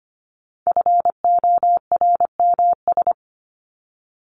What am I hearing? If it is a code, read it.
Morse code, heard as a keyed tone, about 25 wpm, 700 Hz. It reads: FORMS
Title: ES:S Suffix - Top 100 Words with Base Words 25wpm